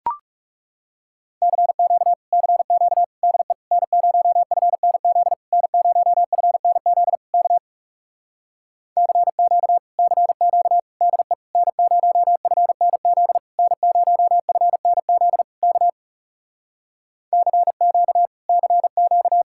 Morse code, heard as a keyed tone, about 37 wpm, 700 Hz. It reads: CQ CQ DE N0FNZ N0FNZ K CQ CQ DE N0FNZ N0FNZ K CQ CQ
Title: Speed-Racing Common QSO Phrases - 30wpm